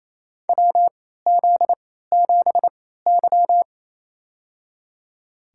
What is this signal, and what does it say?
Morse code, heard as a keyed tone, about 28 wpm, spaced as for 16 wpm, 700 Hz. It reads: WZ7Y
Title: USA Call Signs - 4 Characters - 3x Character Spacing - 28wpm